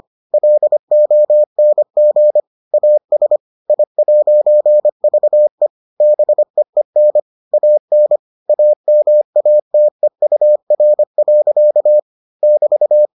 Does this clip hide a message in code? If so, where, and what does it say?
Yes: Morse code from 0.3 to 13.1 s, heard as a keyed tone, about 25 wpm, 600 Hz. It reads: LONG AS I'VE BEEN AN AMATEUR. <BT>